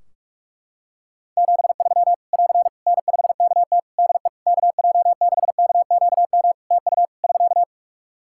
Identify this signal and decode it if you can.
Morse code, heard as a keyed tone, about 46 wpm, 700 Hz. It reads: 73 <AR> N5KT DE KJ6KQM TU <SK>
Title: Practice QSO - Exchange 4 - Ending - 45wpm